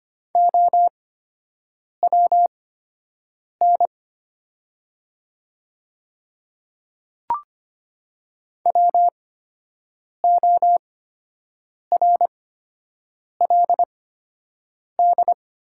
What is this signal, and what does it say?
Morse code, heard as a keyed tone, about 25 wpm, spaced as for 7 wpm, 700 Hz. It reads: OWN WORLD
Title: 8x Character Spacing - Top 300 Words - 25wpm